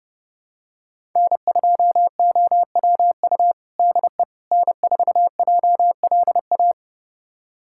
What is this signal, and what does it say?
Morse code, heard as a keyed tone, about 30 wpm, 700 Hz. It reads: N2OWU DE N4JLA
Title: USA Callsign DE Callsign - 5 Characters - 30wpm